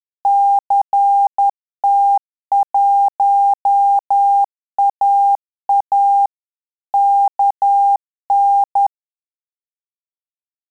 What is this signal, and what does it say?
Morse code, heard as a keyed tone, about 10 wpm, 800 Hz. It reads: CT1AA KN
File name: Telegraphy-group / QSO CW 50 CPM